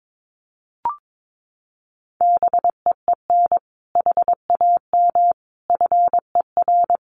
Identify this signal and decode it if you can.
Morse code, heard as a keyed tone, about 22 wpm, 700 Hz. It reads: BEEN HAM FER
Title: Common QSO Phrases 22wpm